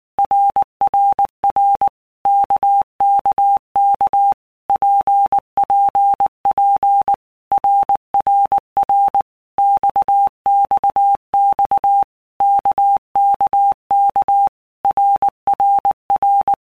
Morse code, heard as a keyed tone, about 19 wpm, 800 Hz. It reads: RRR KKK PPP RRR XXX KKK RRR
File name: All Groups 1-minute Blocks / Groups@18 / RKPX